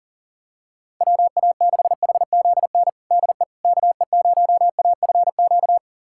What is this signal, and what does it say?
Morse code, heard as a keyed tone, about 37 wpm, 700 Hz. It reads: WA6HZN DE KE0AFQ
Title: USA Callsign DE Callsign - 6 Characters - 40wpm